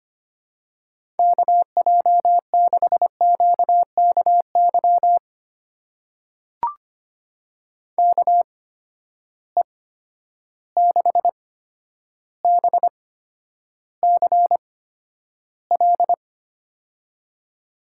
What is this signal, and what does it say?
Morse code, heard as a keyed tone, about 25 wpm, spaced as for 11 wpm, 700 Hz. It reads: KJ6QKY KE6BCL
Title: USA Call Signs - 6 Characters - 8x Character Spacing - 25wpm